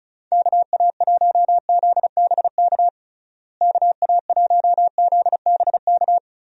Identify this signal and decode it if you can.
Morse code, heard as a keyed tone, about 34 wpm, 700 Hz. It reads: KA1ZBK KA1ZBK
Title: USA Call Signs - 6 Characters - Repeated 3x - 35wpm